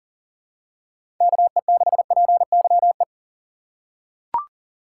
Morse code, heard as a keyed tone, about 40 wpm, 700 Hz. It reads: KE6PYE
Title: USA Call Signs - 6 Characters - 3x Character Spacing - 40wpm